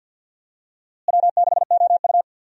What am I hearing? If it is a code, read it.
Morse code, heard as a keyed tone, about 51 wpm, 700 Hz. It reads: W6OU
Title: USA Call Signs - 4 Characters - Repeated 3x - 50wpm